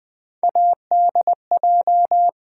Morse code, heard as a keyed tone, about 20 wpm, 700 Hz. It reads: ADJ
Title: Build A Word - 3 Letters At A Time - 6-Letter Words - 20wpm